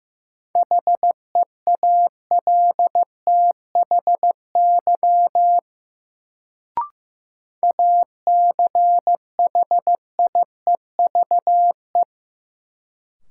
Morse code, heard as a keyed tone, about 15 wpm, 700 Hz. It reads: HEALTHY ACHIEVE
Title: Sentences from Top 1000 Words - Review Part 5 - New Word Drill - 15wpm